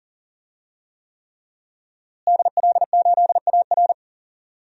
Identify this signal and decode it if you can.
Morse code, heard as a keyed tone, about 40 wpm, 700 Hz. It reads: DL8AR
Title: DXCC-100 Call Signs - 5 Characters - Repeated 3x - 40wpm